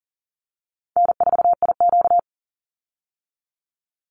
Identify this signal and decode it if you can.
Morse code, heard as a keyed tone, about 40 wpm, 700 Hz. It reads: N4IQ
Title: Call Signs Repeated 3x - 40wpm